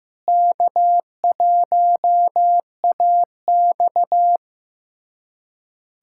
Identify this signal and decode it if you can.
Morse code, heard as a keyed tone, about 15 wpm, 700 Hz. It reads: K1AX
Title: USA Callsign DE Callsign - 4 Characters - 15wpm